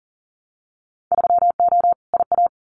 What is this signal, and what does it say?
Morse code, heard as a keyed tone, about 41 wpm, 700 Hz. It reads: 3O IA
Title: Winter Field Day Exchanges 40wpm